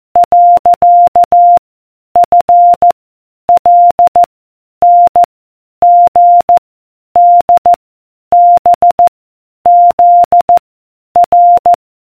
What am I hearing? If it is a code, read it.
Morse code, heard as a keyed tone, about 14 wpm, 700 Hz. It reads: . F L N G D B Z R